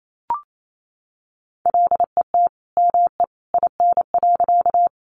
Morse code, heard as a keyed tone, about 28 wpm, 700 Hz. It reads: LET ME IN.
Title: Sentences from Top 300 Words 28wpm